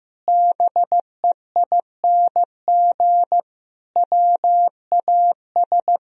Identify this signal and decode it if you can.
Morse code, heard as a keyed tone, about 16 wpm, 700 Hz. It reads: BEING WAS
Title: Sentences from Top 1000 Words - Review Part 4 - 15wpm